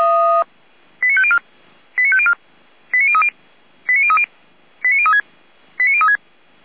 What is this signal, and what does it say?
Five-tone selective call (ZVEI), heard as quick runs of five tones, tones of about 70 ms. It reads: 89504 89504 89030 89030 89036 89036
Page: ZVEI